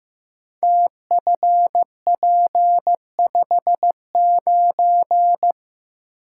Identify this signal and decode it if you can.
Morse code, heard as a keyed tone, about 15 wpm, 700 Hz. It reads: TFP59
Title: Code Groups - 5 Letter:Numbers - 15wpm